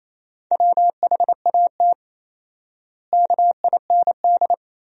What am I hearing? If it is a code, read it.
Morse code, heard as a keyed tone, about 28 wpm, 700 Hz. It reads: WHAT KIND
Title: 4x Word-Spacing - Sentences from Top 500 Words - 28wpm